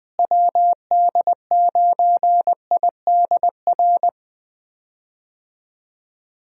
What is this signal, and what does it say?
Morse code, heard as a keyed tone, about 20 wpm, 700 Hz. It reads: WD9IDR
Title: USA Call Signs - 6 Characters - Repeated 2x - 20wpm